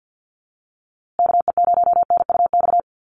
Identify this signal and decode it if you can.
Morse code, heard as a keyed tone, about 48 wpm, 700 Hz. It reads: KE0NUX
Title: USA Call Signs Repeated 3x 50wpm